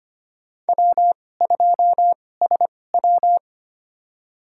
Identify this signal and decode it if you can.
Morse code, heard as a keyed tone, about 25 wpm, spaced as for 18 wpm, 700 Hz. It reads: W2SW
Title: USA Call Signs - 4 Characters - 2x Character Spacing - 25wpm